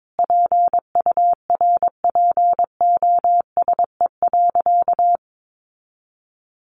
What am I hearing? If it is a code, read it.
Morse code, heard as a keyed tone, about 22 wpm, 700 Hz. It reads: PURPOSE.
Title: Sentences from Top 1400 Words 22wpm